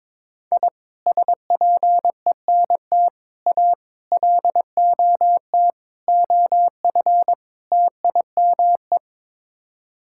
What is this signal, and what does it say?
Morse code, heard as a keyed tone, about 21 wpm, 700 Hz. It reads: I SPENT A LOT OF TIME
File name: practice / 22wpm / Sets of 6 Words - 22wpm